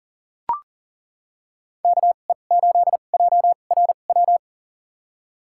Morse code, heard as a keyed tone, about 40 wpm, spaced as for 29 wpm, 700 Hz. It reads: KE8JRW
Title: USA Call Signs - 6 Characters - 2x Character Spacing - 40wpm